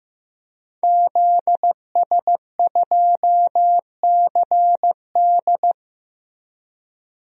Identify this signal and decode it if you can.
Morse code, heard as a keyed tone, about 15 wpm, 700 Hz. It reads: ZS2CD
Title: DXCC-100 Call Signs - 5 Characters - Repeated 2x - 15wpm